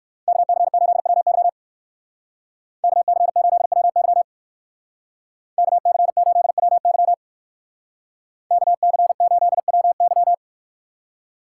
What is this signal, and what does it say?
Morse code, heard as a keyed tone, about 59 wpm, 700 Hz. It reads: KC8WY KC8WY KC8WY KC8WY